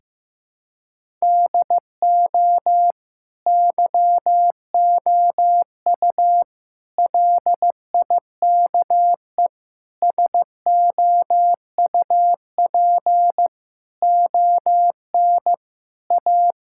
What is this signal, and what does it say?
Morse code, heard as a keyed tone, about 16 wpm, 700 Hz. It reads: DO YOU LIKE SOUP ON A
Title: Sentences from Top 1000 Words - Review Part 6 - 15wpm